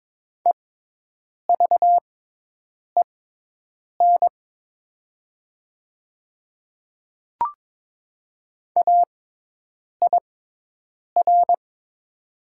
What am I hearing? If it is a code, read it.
Morse code, heard as a keyed tone, about 22 wpm, spaced as for 8 wpm, 700 Hz. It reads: EVEN AIR
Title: 6x Character Spacing - Top 200 Words - 22wpm